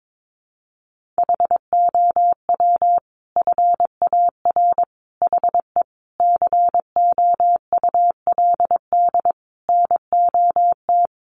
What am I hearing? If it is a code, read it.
Morse code, heard as a keyed tone, about 22 wpm, 700 Hz. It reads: HOW FAR HE COULD NOT
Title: Sentences from Top 1500 Words 22wpm